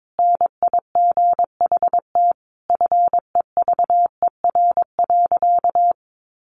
Morse code, heard as a keyed tone, about 22 wpm, 700 Hz. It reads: NIGHT FEVER.
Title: A Sound Of Thunder 22wpm